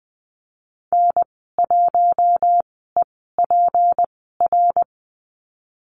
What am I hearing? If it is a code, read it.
Morse code, heard as a keyed tone, about 19 wpm, spaced as for 14 wpm, 700 Hz. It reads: N1EPR
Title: 245 Lesson 39g - Letter z - Call Signs 20wpm